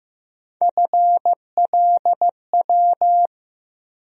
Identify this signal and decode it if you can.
Morse code, heard as a keyed tone, about 15 wpm, 700 Hz. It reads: FLW
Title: USA Call Sign Postfixes - 3 Letters 15wpm